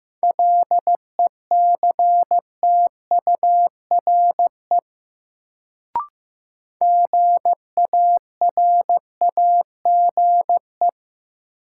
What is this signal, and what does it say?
Morse code, heard as a keyed tone, about 15 wpm, 700 Hz. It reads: LECTURE GARAGE
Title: Sentences from Top 1600 Words - New Word Drill Alt - 15wpm